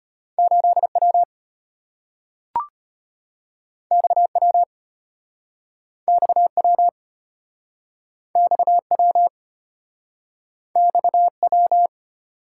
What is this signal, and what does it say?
Morse code, heard as a keyed tone, about 35 wpm, 700 Hz. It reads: 8W XW XW XW XW